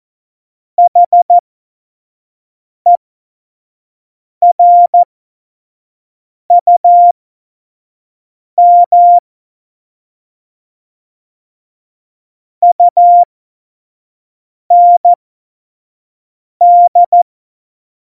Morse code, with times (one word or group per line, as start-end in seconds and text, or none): 0.8-9.2 HERUM
12.6-17.2 UND